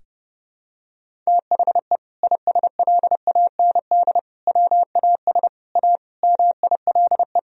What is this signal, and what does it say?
Morse code, heard as a keyed tone, about 30 wpm, 700 Hz. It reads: THE ISLAND WAS A MILE
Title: Sentences from Top 500 Words and Abbreviations 30wpm